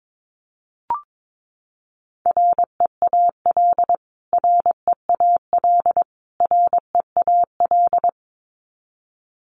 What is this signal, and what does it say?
Morse code, heard as a keyed tone, about 22 wpm, 700 Hz. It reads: REAL REAL REAL
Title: Top 500 Words Repeated 22wpm